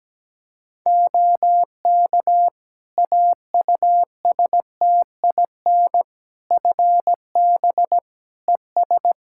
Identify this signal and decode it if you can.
Morse code, heard as a keyed tone, about 17 wpm, 700 Hz. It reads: OK AUSTIN FB ES
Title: Practice QSO - Exchange 2 - Info - 17wpm